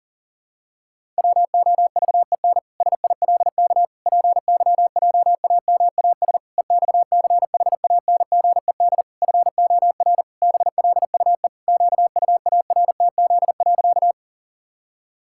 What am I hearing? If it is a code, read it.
Morse code, heard as a keyed tone, about 40 wpm, 700 Hz. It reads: WOVEN SILK PYJAMAS EXCHANGED FOR BLUE QUARTZ.
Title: Pangram Challenge v2 - 40wpm